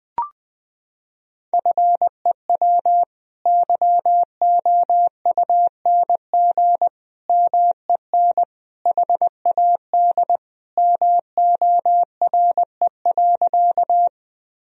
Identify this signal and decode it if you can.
Morse code, heard as a keyed tone, about 20 wpm, 700 Hz. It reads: FEW YOUNG MEN HAD MORE.
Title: Sentences from Top 1000 Words - Review Part 1 - 20wpm